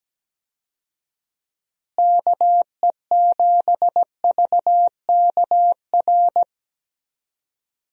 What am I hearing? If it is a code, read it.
Morse code, heard as a keyed tone, about 17 wpm, 700 Hz. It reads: KE7VKR
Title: USA Call Signs - 6 Characters - Repeated 3x - 17wpm